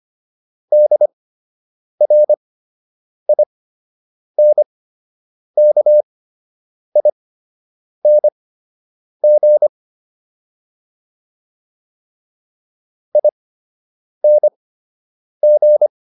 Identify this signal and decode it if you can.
Morse code, heard as a keyed tone, about 25 wpm, spaced as for 8 wpm, 600 Hz. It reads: DRINKING ING